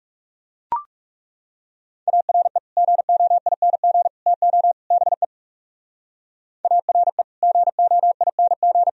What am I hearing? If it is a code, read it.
Morse code, heard as a keyed tone, about 36 wpm, 700 Hz. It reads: ARE GOING TO BE ARE GOING
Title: Speed-Racing - Sets of 4 Words 30wpm